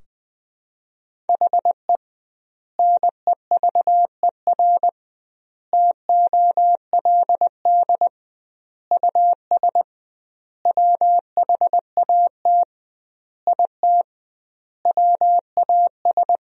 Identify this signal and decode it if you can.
Morse code, heard as a keyed tone, about 20 wpm, 700 Hz. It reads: HE NEVER TOLD US WHAT IT WAS
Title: 2x Word-Spacing - Sentences from Top 400 Words - 20wpm